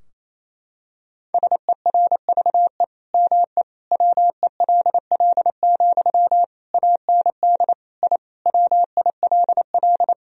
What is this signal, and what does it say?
Morse code, heard as a keyed tone, about 28 wpm, 700 Hz. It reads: SERVE ME WELL, AND I WILL
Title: Sentences from Top 1000 Words - Review Part 2 - 28wpm